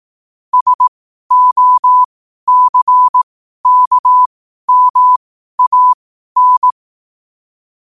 Morse code, heard as a keyed tone, about 18 wpm, spaced as for 12 wpm, 1000 Hz. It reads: SOCKMAN
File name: qotdfiles / 12wpm